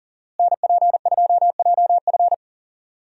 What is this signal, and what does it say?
Morse code, heard as a keyed tone, about 37 wpm, 700 Hz. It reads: NP2JF
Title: USA Call Signs - 5 Characters - Repeated 2x - 40wpm